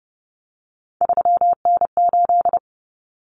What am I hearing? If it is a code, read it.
Morse code, heard as a keyed tone, about 30 wpm, 700 Hz. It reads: 3N8
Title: Common Cut Numbers - 3-Digit 30wpm